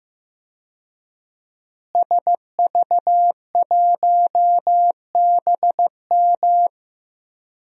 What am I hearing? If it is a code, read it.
Morse code, heard as a keyed tone, about 15 wpm, 700 Hz. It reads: SV1BM
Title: DXCC-100 Call Signs - 5 Characters - Repeated 3x - 15wpm